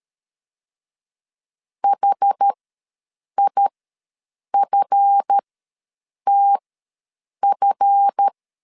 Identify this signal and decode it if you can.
Morse code, heard as a keyed tone, about 13 wpm, spaced as for 7 wpm, 800 Hz. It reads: HIFTF
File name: LEZIONI / 7 GF9Z / 7 GF ws 12wpm 3140